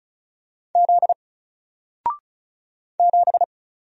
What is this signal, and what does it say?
Morse code, heard as a keyed tone, about 35 wpm, 700 Hz. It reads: Z 7
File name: Single-Letter-Number-35wpm